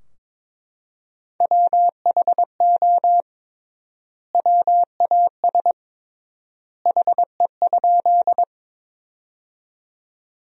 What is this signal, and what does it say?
Morse code, heard as a keyed tone, about 22 wpm, 700 Hz. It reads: WHO WAS HE?
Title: 3x Word-Spacing - Sentences from Top 100 Words - 22wpm